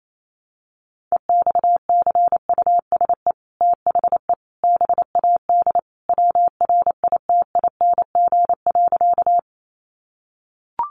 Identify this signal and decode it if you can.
Morse code, heard as a keyed tone, about 28 wpm, 700 Hz. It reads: EXCUSE THE BAD WRITING.
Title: Sentences from Top 800 Words 28wpm